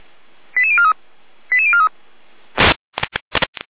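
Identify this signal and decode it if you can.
Five-tone selective call (ZVEI), heard as quick runs of five tones, tones of about 80 ms. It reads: 88043 88043
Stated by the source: ZVEI five-tone selective call received by SDR, channel 455.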